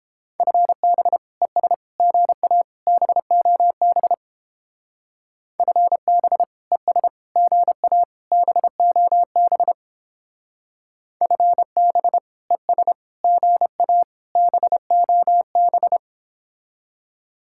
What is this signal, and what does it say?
Morse code, heard as a keyed tone, about 31 wpm, 700 Hz. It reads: FB ES GA BOB FB ES GA BOB FB ES GA BOB